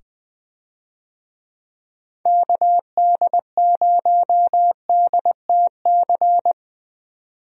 Morse code, heard as a keyed tone, about 20 wpm, 700 Hz. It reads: KD0DTC
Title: USA Call Signs - 6 Characters - 20wpm